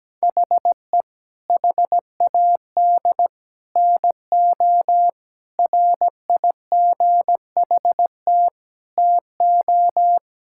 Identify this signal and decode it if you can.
Morse code, heard as a keyed tone, about 17 wpm, 700 Hz. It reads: HE HAD NO RIGHT TO
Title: Sentences from Top 1000 Words - Review Part 5 - 17wpm